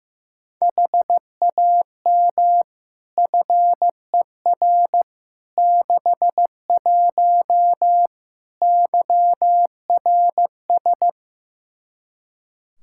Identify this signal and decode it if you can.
Morse code, heard as a keyed tone, about 16 wpm, 700 Hz. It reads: HAM FER 61 YRS
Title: Practice QSO - Exchange 3 - More Info - 15wpm